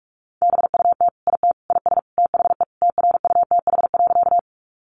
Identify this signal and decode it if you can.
Morse code, heard as a keyed tone, about 44 wpm, 700 Hz. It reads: BUT IT IS THE TRUTH.
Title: Sentences from Top 800 Words 45wpm